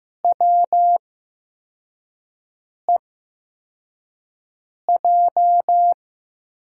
Morse code, heard as a keyed tone, about 15 wpm, spaced as for 4 wpm, 700 Hz. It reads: WEJ
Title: USA Call Signs - 6 Characters - 8x Character Spacing - 15wpm